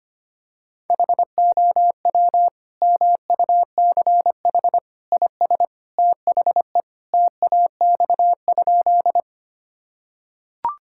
Morse code, heard as a keyed tone, about 25 wpm, 700 Hz. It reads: HOW MUCH IS THE TAX?
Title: Sentences from Top 1000 Words - Review Part 5 - 25wpm